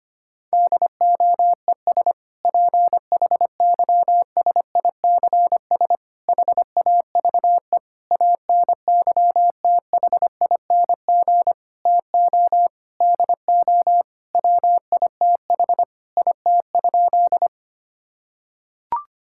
Morse code, heard as a keyed tone, about 25 wpm, 700 Hz. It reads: DOES PHYSICS HAVE ANYTHING TO DO WITH IT?
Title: Sentences from Top 1000 Words - Review Part 5 - 25wpm